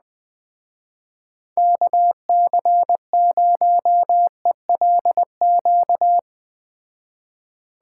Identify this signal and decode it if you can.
Morse code, heard as a keyed tone, about 20 wpm, 700 Hz. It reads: KC0ELQ